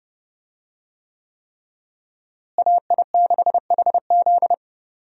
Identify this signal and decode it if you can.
Morse code, heard as a keyed tone, about 30 wpm, 700 Hz. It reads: AI6HZ
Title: USA Call Signs - 5 Characters - Repeated 3x - 30wpm